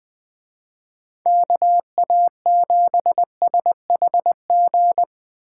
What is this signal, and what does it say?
Morse code, heard as a keyed tone, about 20 wpm, 700 Hz. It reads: KA7SHG
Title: Speed Racing - USA 2x3 Call Signs 17wpm